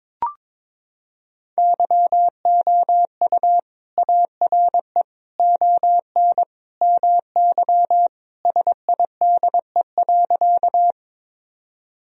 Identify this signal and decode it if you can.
Morse code, heard as a keyed tone, about 22 wpm, 700 Hz. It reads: YOU ARE ON MY SIDE.
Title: Sentences from Top 1000 Words - Review Part 1 - 22wpm